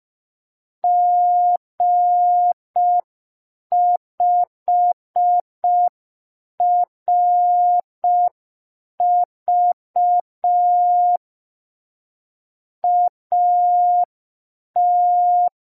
Morse code, heard as a keyed tone, about 5 wpm, 710 Hz. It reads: G5RV AT
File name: g3zpd / Morse code P2